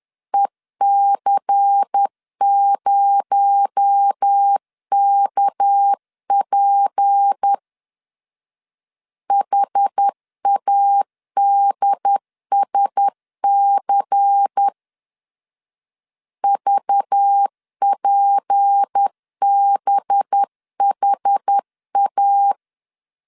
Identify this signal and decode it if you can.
Morse code, heard as a keyed tone, about 10 wpm, 800 Hz. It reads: EC0KP HADSC VPBHA
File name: CW / DET / 1 6 1500C S B 10wpm 4137